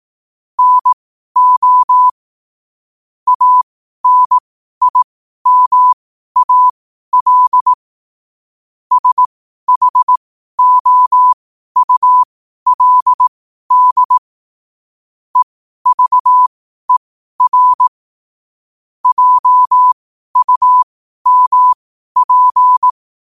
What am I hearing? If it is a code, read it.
Morse code, heard as a keyed tone, about 18 wpm, spaced as for 12 wpm, 1000 Hz. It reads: NO ANIMAL SHOULD EVER JUMP
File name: qotdfiles / 12wpm